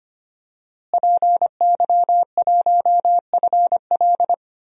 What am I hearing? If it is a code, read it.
Morse code, heard as a keyed tone, about 25 wpm, 700 Hz. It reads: PY1FL
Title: DXCC-100 Call Signs - 5 Characters - Repeated 2x - 25wpm